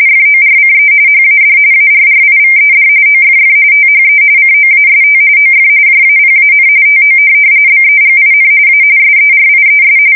RTTY, about 45.45 baud. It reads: FORCE HAD ARRIVED OFF THE COAST OF JAPAN TO BEGIN A SERIES
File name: fsk170 / USS-Missouri